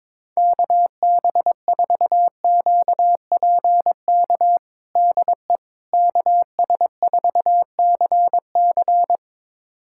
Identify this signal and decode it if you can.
Morse code, heard as a keyed tone, about 22 wpm, 700 Hz. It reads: KB4QPK DE KS4CC
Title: USA Callsign DE Callsign 22wpm